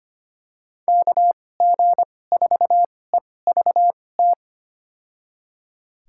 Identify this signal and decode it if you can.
Morse code, heard as a keyed tone, about 25 wpm, spaced as for 18 wpm, 700 Hz. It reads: KG4EVT
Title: USA Call Signs - 6 Characters - 2x Character Spacing - 25wpm